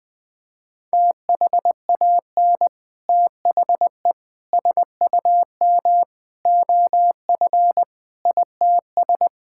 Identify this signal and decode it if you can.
Morse code, heard as a keyed tone, about 20 wpm, 700 Hz. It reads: THAN THE SUM OF ITS
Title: Sets of 6 Words - 20wpm